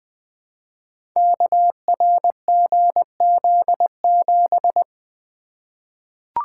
Morse code, heard as a keyed tone, about 20 wpm, 700 Hz. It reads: KRGZ7
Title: Code Groups - 5 Letter:Numbers - 20wpm